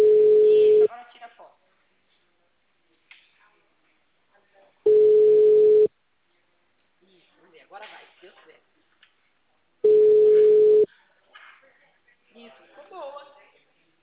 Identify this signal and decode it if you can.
Ringback tone (Europe, 425 Hz), with speech.
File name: out- -2002- - - .92883